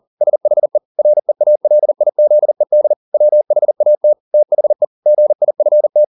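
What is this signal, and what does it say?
Morse code, heard as a keyed tone, about 40 wpm, 600 Hz. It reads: SHE REALIZED WHAT THE GIFT
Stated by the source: Morse code at 40 wpm, practice text, CW med Gnister.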